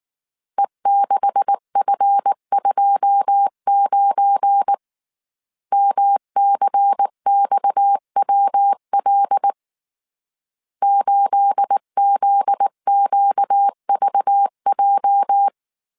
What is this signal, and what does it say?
Morse code, heard as a keyed tone, about 19 wpm, 800 Hz. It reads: E6F29 MCXWL 8ZQVJ
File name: CW / DET / 1 9 1500C S A 18wpm 2459